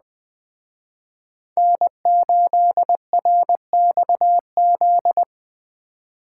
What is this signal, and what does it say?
Morse code, heard as a keyed tone, about 21 wpm, 700 Hz. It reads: N8RXZ